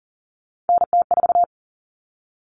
Morse code, heard as a keyed tone, about 40 wpm, 700 Hz. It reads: NT4
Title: All Cut Numbers - 3-Digit 40wpm